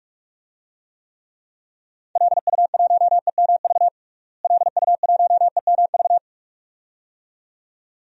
Morse code, heard as a keyed tone, about 43 wpm, 700 Hz. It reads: LU1EMV LU1EMV
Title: DXCC-100 Call Signs - 6 Characters - Repeated 2x - 45wpm